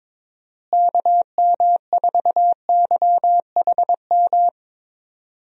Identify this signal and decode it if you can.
Morse code, heard as a keyed tone, about 22 wpm, 700 Hz. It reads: KM4YHM
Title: USA Call Signs - 6 Characters - Repeated 3x - 22wpm